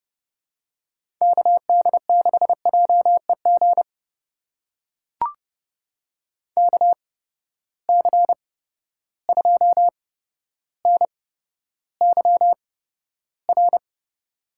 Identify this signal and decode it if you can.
Morse code, heard as a keyed tone, about 30 wpm, spaced as for 13 wpm, 700 Hz. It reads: KD6JEG KC2NYR
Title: USA Call Signs - 6 Characters - 8x Character Spacing - 30wpm